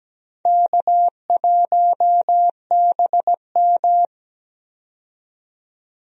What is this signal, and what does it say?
Morse code, heard as a keyed tone, about 17 wpm, 700 Hz. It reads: K1BM
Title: USA Call Signs - 4 Characters - 17wpm